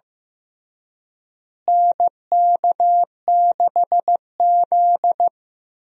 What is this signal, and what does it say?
Morse code, heard as a keyed tone, about 15 wpm, 700 Hz. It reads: NK6Z
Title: USA Call Signs - 4 Characters - Repeated 3x - 15wpm